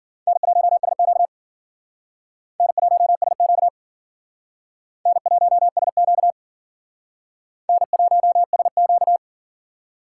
Morse code, heard as a keyed tone, about 52 wpm, 700 Hz. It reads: N1SQ N1SQ N1SQ N1SQ